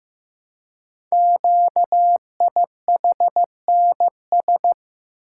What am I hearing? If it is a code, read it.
Morse code, heard as a keyed tone, about 15 wpm, 700 Hz. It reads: QIHNS
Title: Code Groups - 5 Letters - 15wpm